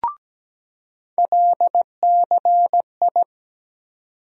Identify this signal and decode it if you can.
Morse code, heard as a keyed tone, about 17 wpm, 700 Hz. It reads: LCI
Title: USA Call Sign Postfixes - 3 Letters 17wpm